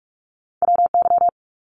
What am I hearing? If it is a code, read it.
Morse code, heard as a keyed tone, about 44 wpm, 700 Hz. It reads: WY
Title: ARRL Sections 45wpm